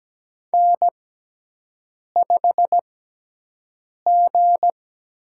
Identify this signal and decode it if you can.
Morse code, heard as a keyed tone, about 17 wpm, spaced as for 6 wpm, 700 Hz. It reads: N5G
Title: USA Call Signs - 4 Characters - 6x Character Spacing - 17wpm